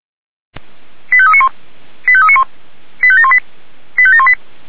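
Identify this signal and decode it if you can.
Five-tone selective call (ZVEI), heard as quick runs of five tones, tones of about 70 ms. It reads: 85391 85391 85718 85718
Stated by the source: ZVEI five-tone selective call received on an SDR.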